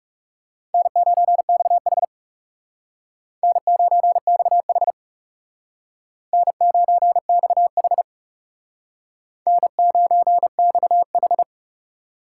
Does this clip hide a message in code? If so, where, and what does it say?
Yes: Morse code from 0.7 to 11.4 s, heard as a keyed tone, about 38 wpm, 700 Hz. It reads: N9XH N9XH N9XH N9XH